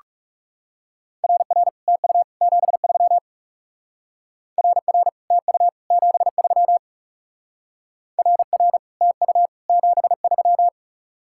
Morse code, heard as a keyed tone, about 38 wpm, 700 Hz. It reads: RR TU 73 RR TU 73 RR TU 73